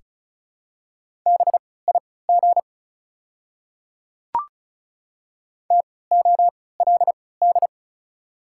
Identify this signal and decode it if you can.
Morse code, heard as a keyed tone, about 34 wpm, spaced as for 20 wpm, 700 Hz. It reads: BIG TOLD
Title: 3x Character Spacing - Top 500 Words - 35wpm